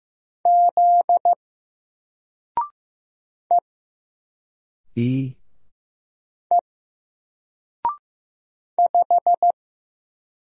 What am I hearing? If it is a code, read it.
Morse code, heard as a keyed tone, about 15 wpm, 700 Hz. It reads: Z E E 5
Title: Single Letter-Number 15wpm